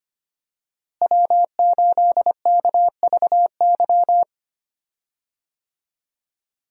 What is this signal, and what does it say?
Morse code, heard as a keyed tone, about 25 wpm, 700 Hz. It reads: W8KVY
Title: USA Call Signs - 5 Characters - Repeated 2x - 25wpm